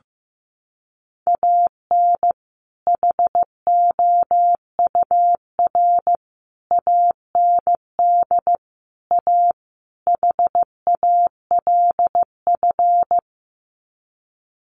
Morse code, heard as a keyed tone, about 14 wpm, 700 Hz. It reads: AN HOUR AND A HALF
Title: Sets of 5 Words 15wpm